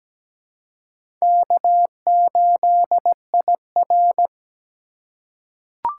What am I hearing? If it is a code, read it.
Morse code, heard as a keyed tone, about 17 wpm, 700 Hz. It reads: K8IR
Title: USA Call Signs - 4 Characters - 4x Character Spacing - 17wpm